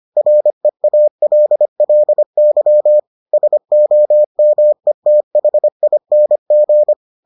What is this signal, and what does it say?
Morse code, heard as a keyed tone, about 25 wpm, 600 Hz. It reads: REALLY SOMETHING